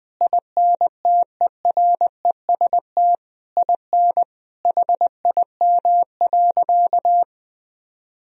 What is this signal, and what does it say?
Morse code, heard as a keyed tone, about 20 wpm, 700 Hz. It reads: INTEREST IN HIM.
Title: Sentences from Top 1000 Words - Review Part 3 - 20wpm